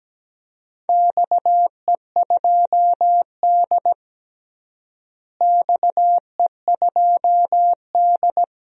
Morse code, heard as a keyed tone, about 17 wpm, 700 Hz. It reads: XE2D XE2D
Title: DXCC-100 Call Signs - 4 Characters - Repeated 3x - 17wpm